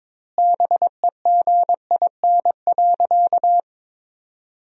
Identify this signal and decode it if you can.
Morse code, heard as a keyed tone, about 22 wpm, 700 Hz. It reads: BEGIN.
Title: 2x Word-Spacing - Sentences from Top 300 Words - 22wpm